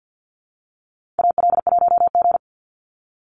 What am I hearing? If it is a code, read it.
Morse code, heard as a keyed tone, about 50 wpm, 700 Hz. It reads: AL1G
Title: Call Signs Repeated 2x - 50wpm